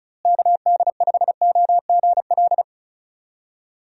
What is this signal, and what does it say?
Morse code, heard as a keyed tone, about 34 wpm, 700 Hz. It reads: KD5OGL
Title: USA Call Signs - 6 Characters - Repeated 3x - 35wpm